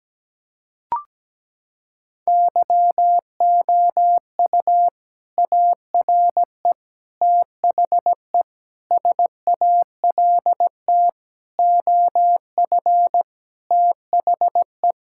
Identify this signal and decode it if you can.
Morse code, heard as a keyed tone, about 17 wpm, 700 Hz. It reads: YOU ARE THE SALT OF THE
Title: Sentences from Top 1000 Words - Review Part 5 - 17wpm